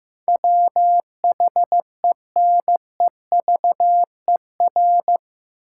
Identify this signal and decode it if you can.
Morse code, heard as a keyed tone, about 16 wpm, 700 Hz. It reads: WHENEVER
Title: Sentences from Top 1200 Words - New Word Drill Alt - 15wpm